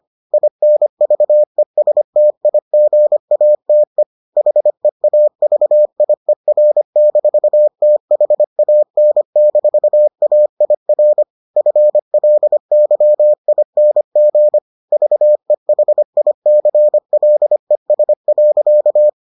Morse code, heard as a keyed tone, about 25 wpm, 600 Hz. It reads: INVESTIGATE HEAVIER-THAN-AIR FLYING VEHICLES.